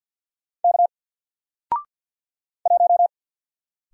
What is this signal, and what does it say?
Morse code, heard as a keyed tone, about 48 wpm, 700 Hz. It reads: K 1
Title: Single Letter-Number Alt 50wpm